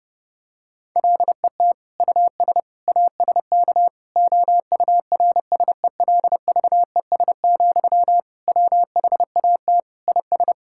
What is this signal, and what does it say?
Morse code, heard as a keyed tone, about 30 wpm, 700 Hz. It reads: LET US ASK OURSELVES, WHAT IS
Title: Sentences from Top 1600 Words 30wpm